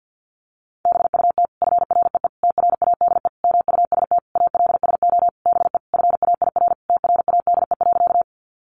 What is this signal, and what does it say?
Morse code, heard as a keyed tone, about 50 wpm, 700 Hz. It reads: BUT FREE TRADE MUST ALSO BE FAIR TRADE.
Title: Sentences from Top 1300 Words 50wpm